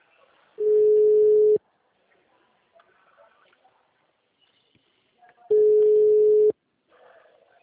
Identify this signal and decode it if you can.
Ringback tone (Europe, 425 Hz).
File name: out- -2013- - - .90332